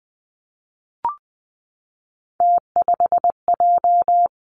Morse code, heard as a keyed tone, about 20 wpm, 700 Hz. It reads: T5J
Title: Three Challenging Letter-Number Combinations 20wpm